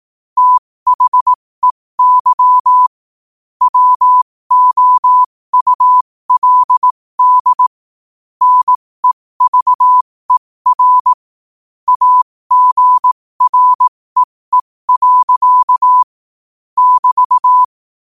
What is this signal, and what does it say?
Morse code, heard as a keyed tone, about 18 wpm, spaced as for 15 wpm, 1000 Hz. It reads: THEY WOULD NEVER AGREE. <BT>
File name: qotdfiles / 15wpm